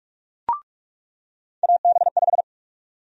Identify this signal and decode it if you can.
Morse code, heard as a keyed tone, about 46 wpm, 700 Hz. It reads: AB5
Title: USA Call Sign Prefixes - 45wpm